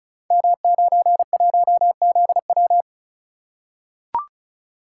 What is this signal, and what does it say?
Morse code, heard as a keyed tone, about 35 wpm, 700 Hz. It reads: M91ZW
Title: Code Groups - 5 Letter:Numbers - 35wpm